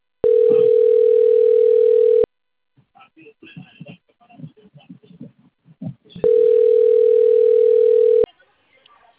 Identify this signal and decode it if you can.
Ringback tone (North America), 440 + 480 Hz, with speech.
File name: rg-1600-unknown- - - .12590